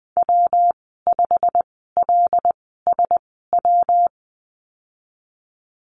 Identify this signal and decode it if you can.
Morse code, heard as a keyed tone, about 20 wpm, spaced as for 15 wpm, 700 Hz. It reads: W5LSW
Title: 140 Lesson 23g - Letter f - Call Signs 20wpm